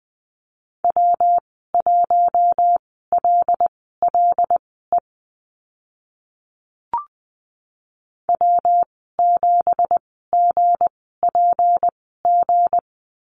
Morse code, heard as a keyed tone, about 20 wpm, spaced as for 15 wpm, 700 Hz. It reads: W1LLE W7GPG
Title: 173 Lesson 28g - Number 9 - Call Signs 20wpm